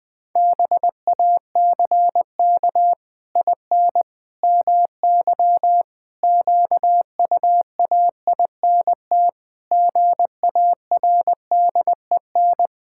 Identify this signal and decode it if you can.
Morse code, heard as a keyed tone, about 19 wpm, 700 Hz. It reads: BACK IN MY QUAINT GARDEN
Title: Pangram Challenge v2 - 20wpm